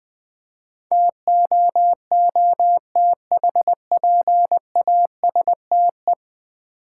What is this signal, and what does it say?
Morse code, heard as a keyed tone, about 20 wpm, 700 Hz. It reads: TOOTHPASTE
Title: Easy Words with Hints - 20wpm